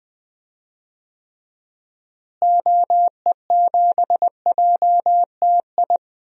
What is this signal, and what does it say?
Morse code, heard as a keyed tone, about 20 wpm, 700 Hz. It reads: OE7JTI
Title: DXCC-100 Call Signs - 6 Characters - Repeated 3x - 20wpm